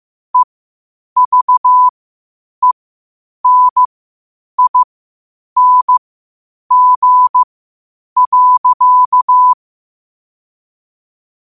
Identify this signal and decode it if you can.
Morse code, heard as a keyed tone, about 15 wpm, spaced as for 8 wpm, 1000 Hz. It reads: EVENING.
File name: qotdfiles / 8wpm